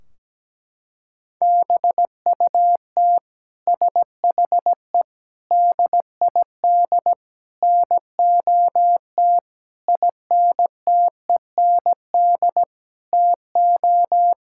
Morse code, heard as a keyed tone, about 17 wpm, 700 Hz. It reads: BUT SHE DID NOT INTEND TO
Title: Sentences from Top 1600 Words 17wpm